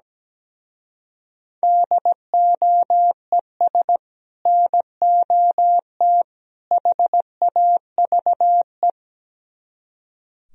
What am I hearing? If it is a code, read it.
Morse code, heard as a keyed tone, about 17 wpm, 700 Hz. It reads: DOES NOT HAVE
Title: Speed-Racing - Sets of 3 Words 17wpm